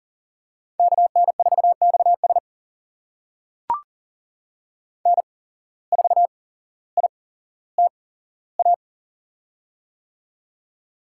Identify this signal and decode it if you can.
Morse code, heard as a keyed tone, about 39 wpm, spaced as for 17 wpm, 700 Hz. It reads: KN4XS N4ITA
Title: USA Call Signs - 5 Characters - 8x Character Spacing - 40wpm